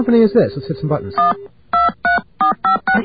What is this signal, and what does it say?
DTMF, heard as telephone touch tones, tones of about 130 ms, with speech. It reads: 1 23456